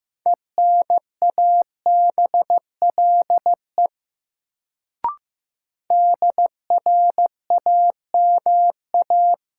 Morse code, heard as a keyed tone, about 15 wpm, 700 Hz. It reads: ENABLE DRAMA
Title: Sentences from Top 1500 Words - New Word Drill Alt - 15wpm